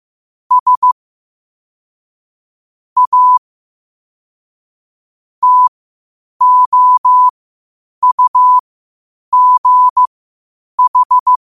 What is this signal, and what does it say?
Morse code, heard as a keyed tone, about 15 wpm, spaced as for 8 wpm, 1000 Hz. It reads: S A TOUGH
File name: qotdfiles / 8wpm